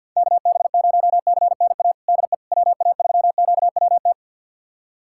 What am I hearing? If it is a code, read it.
Morse code, heard as a keyed tone, about 46 wpm, 700 Hz. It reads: KB0CNA DE WA3QWT
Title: USA Callsign DE Callsign - 6 Characters - 50wpm